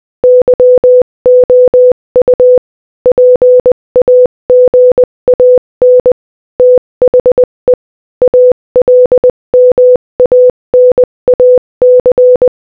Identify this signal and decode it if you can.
Morse code, heard as a keyed tone, about 20 wpm, 500 Hz. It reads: YOU PAGAN THE ALMANAC
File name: Morse-Code-Audio-Speed-20-9